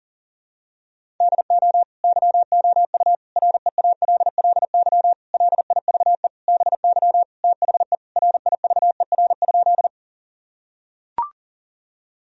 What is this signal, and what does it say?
Morse code, heard as a keyed tone, about 40 wpm, 700 Hz. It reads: DO YOU REALLY LIVE BY THE RIVER?
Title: Sentences from Top 1000 Words - Review Part 1 - 40wpm